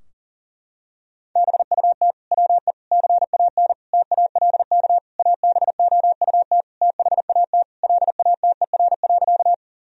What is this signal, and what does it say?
Morse code, heard as a keyed tone, about 40 wpm, 700 Hz. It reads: BUT WE CAN TALK ABOUT THAT LATER.
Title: Sentences from Top 1000 Words - Review Part 4 - 40wpm